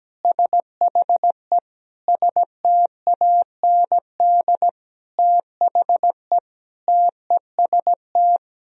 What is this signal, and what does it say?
Morse code, heard as a keyed tone, about 17 wpm, 700 Hz. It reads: SHE STAND THE TEST
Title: Sentences from Top 1000 Words - Review Part 3 - 17wpm